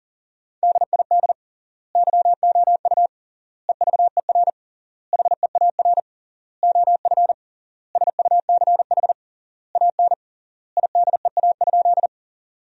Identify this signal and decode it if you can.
Morse code, heard as a keyed tone, about 40 wpm, 700 Hz. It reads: DID YOU EVER HEAR OF SUCH AN IDEA?
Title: 3x Word-Spacing - Sentences from Top 400 Words - 40wpm